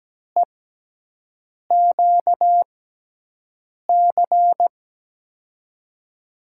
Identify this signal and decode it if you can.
Morse code, heard as a keyed tone, about 17 wpm, spaced as for 6 wpm, 700 Hz. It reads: EQC
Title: USA Call Signs - 6 Characters - 6x Character Spacing - 17wpm